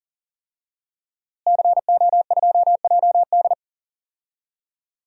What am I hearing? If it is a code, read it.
Morse code, heard as a keyed tone, about 40 wpm, 700 Hz. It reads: CO2JD
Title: DXCC-100 Call Signs - 5 Characters - Repeated 3x - 40wpm